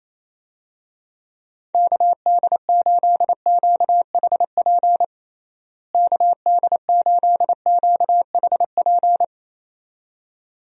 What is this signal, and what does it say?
Morse code, heard as a keyed tone, about 27 wpm, 700 Hz. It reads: KD8QHP KD8QHP
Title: USA Call Signs - 6 Characters - Repeated 2x - 28wpm